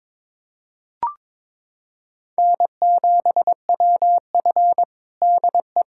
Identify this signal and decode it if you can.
Morse code, heard as a keyed tone, about 21 wpm, 700 Hz. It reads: N7WF DE
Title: USA Callsign DE Callsign - 4 Characters - 22wpm